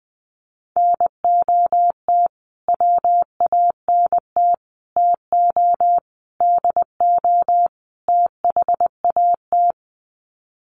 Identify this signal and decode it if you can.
Morse code, heard as a keyed tone, about 20 wpm, 700 Hz. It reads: NOT WANT TO DO THAT
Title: Sets of 5 Words Encore 20wpm